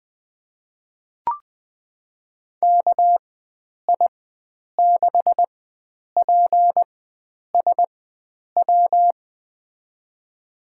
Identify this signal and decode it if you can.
Morse code, heard as a keyed tone, about 20 wpm, spaced as for 9 wpm, 700 Hz. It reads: KI6PSW
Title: USA Call Signs - 4x Character Spacing - 20wpm